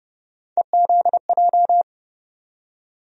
Morse code, heard as a keyed tone, about 30 wpm, 700 Hz. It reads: EZJ
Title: USA Call Sign Postfixes - 3 Letters 30wpm